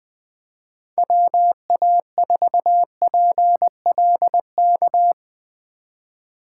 Morse code, heard as a keyed tone, about 20 wpm, 700 Hz. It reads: WA4PLK
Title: Speed Racing - USA 2x3 Call Signs 20wpm